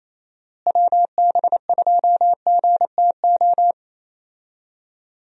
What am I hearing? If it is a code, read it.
Morse code, heard as a keyed tone, about 28 wpm, 700 Hz. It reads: WB2GTO
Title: USA Call Signs - 6 Characters - Repeated 3x - 28wpm